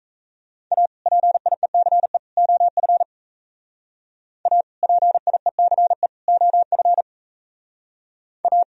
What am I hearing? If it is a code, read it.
Morse code, heard as a keyed tone, about 37 wpm, 700 Hz. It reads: A PIECE OF A PIECE OF A